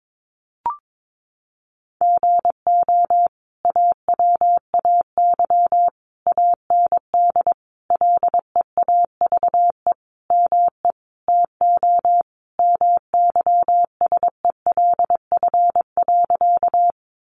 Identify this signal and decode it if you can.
Morse code, heard as a keyed tone, about 22 wpm, 700 Hz. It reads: GO AWAY AND LEAVE ME TO MYSELF.
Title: Sentences from Top 1300 Words 22wpm